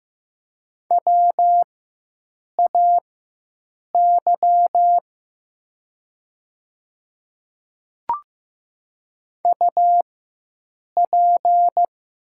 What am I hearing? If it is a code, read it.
Morse code, heard as a keyed tone, about 15 wpm, spaced as for 7 wpm, 700 Hz. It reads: WAY UP
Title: 4x Character Spacing - Top 100 Words - 15wpm